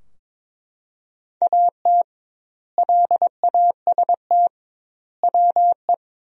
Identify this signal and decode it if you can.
Morse code, heard as a keyed tone, about 22 wpm, 700 Hz. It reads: AT LAST WE
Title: 2x Word-Spacing - Sentences from Top 300 Words - 22wpm